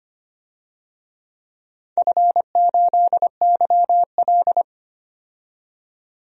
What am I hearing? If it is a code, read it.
Morse code, heard as a keyed tone, about 25 wpm, 700 Hz. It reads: F8YL
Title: DXCC-100 Call Signs - 4 Characters - 25wpm